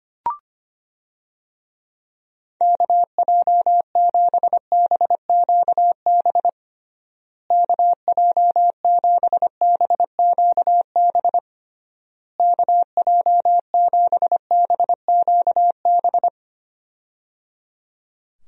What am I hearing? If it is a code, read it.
Morse code, heard as a keyed tone, about 25 wpm, 700 Hz. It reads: KJ7BQB KJ7BQB KJ7BQB
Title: USA Call Signs - 6 Characters - Repeated 3x - 25wpm